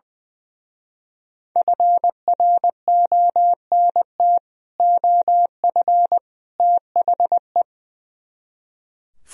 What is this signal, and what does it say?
Morse code, heard as a keyed tone, about 22 wpm, 700 Hz. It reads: FRONT OF THE
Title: Speed-Racing - Sets of 3 Words 20wpm